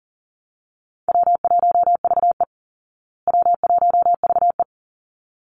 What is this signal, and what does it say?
Morse code, heard as a keyed tone, about 40 wpm, 700 Hz. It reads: W1VE W1VE
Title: Easy Call Signs Repeated 2x - 40wpm